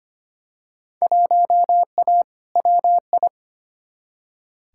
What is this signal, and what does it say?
Morse code, heard as a keyed tone, about 25 wpm, 700 Hz. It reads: 1A WI
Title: Field Day Exchanges - 25wpm